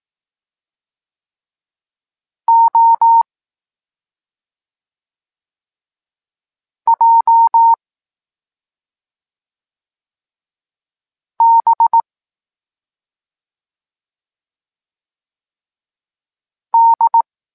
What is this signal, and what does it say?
Morse code, heard as a keyed tone, about 18 wpm, 920 Hz. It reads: O J B D
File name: Tekst-26 1